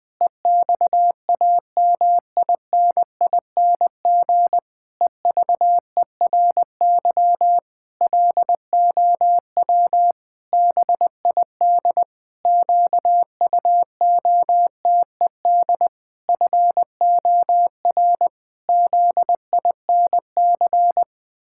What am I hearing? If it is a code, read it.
Morse code, heard as a keyed tone, about 21 wpm, 700 Hz. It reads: EXAMINING EVERY LOW BID QUOTED FOR ZINC
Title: Pangram Challenge v2 - 20wpm